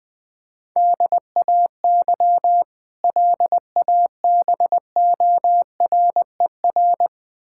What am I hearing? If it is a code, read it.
Morse code, heard as a keyed tone, about 20 wpm, 700 Hz. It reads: DAY LABORER
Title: K1USN SST QSO Practice 20wpm